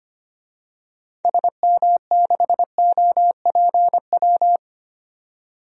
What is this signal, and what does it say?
Morse code, heard as a keyed tone, about 25 wpm, 700 Hz. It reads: SM6OPW
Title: DXCC-100 Call Signs - 6 Characters - 25wpm